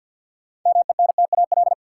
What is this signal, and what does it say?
Morse code, heard as a keyed tone, about 50 wpm, 700 Hz. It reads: MENTAL
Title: Sentences from Top 1000 Words - Review Part 4 - New Word Drill - 50wpm